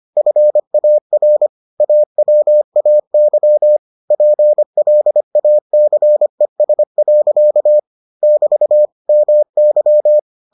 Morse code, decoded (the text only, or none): FAR AWAY PLACES. <BT> MY